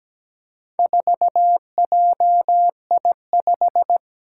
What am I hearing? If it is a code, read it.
Morse code, heard as a keyed tone, about 17 wpm, 700 Hz. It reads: 4JI5
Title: Four Challenging Letter-Number Combinations 17wpm